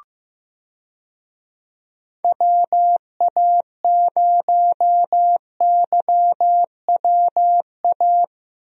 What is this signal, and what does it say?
Morse code, heard as a keyed tone, about 15 wpm, 700 Hz. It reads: WA0YWA
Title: USA Call Signs - 6 Characters - Repeated 2x - 15wpm